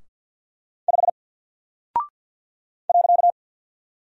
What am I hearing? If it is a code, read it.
Morse code, heard as a keyed tone, about 48 wpm, 700 Hz. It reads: 5 .
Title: Single Character (Letter:Number:Punctuation:Prosign) 50wpm